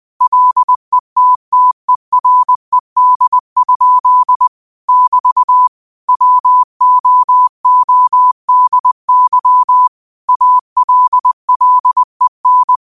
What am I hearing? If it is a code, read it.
Morse code, heard as a keyed tone, about 20 wpm, 1000 Hz. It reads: LETTERED? <BT> WOODY ALLEN
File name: qotdfiles / 20wpm